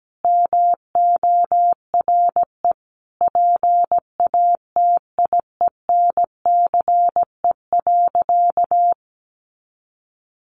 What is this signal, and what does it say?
Morse code, heard as a keyed tone, about 17 wpm, 700 Hz. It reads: MORE PATIENCE.
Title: Sentences from Top 1300 Words 17wpm